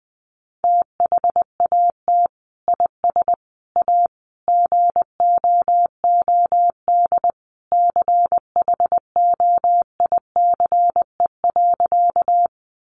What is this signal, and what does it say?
Morse code, heard as a keyed tone, about 20 wpm, 700 Hz. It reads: THAT IS A GOOD CHOICE.
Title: Sentences from Top 900 Words 20wpm